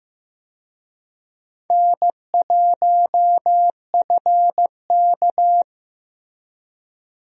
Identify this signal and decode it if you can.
Morse code, heard as a keyed tone, about 15 wpm, 700 Hz. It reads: N1FK
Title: USA Call Signs - 4 Characters - Repeated 3x - 15wpm